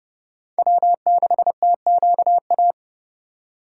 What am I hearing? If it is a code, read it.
Morse code, heard as a keyed tone, about 31 wpm, 700 Hz. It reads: W6TQA
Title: USA Call Signs - 4x Character Spacing - 30wpm